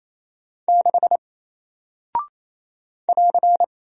Morse code, heard as a keyed tone, about 27 wpm, 700 Hz. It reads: 6 <AR>
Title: Single Character (Letter:Number:Punctuation:Prosign) Alt 28wpm